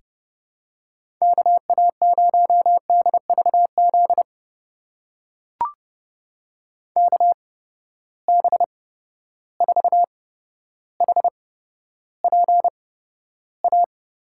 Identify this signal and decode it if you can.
Morse code, heard as a keyed tone, about 30 wpm, spaced as for 13 wpm, 700 Hz. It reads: KA0DVZ KB4HPA
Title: USA Call Signs - 6 Characters - 8x Character Spacing - 30wpm